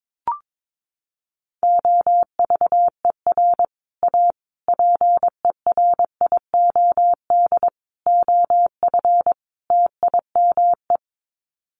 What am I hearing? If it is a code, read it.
Morse code, heard as a keyed tone, about 22 wpm, 700 Hz. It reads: OVER A PERIOD OF TIME
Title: Sets of 5 Words Encore2 22wpm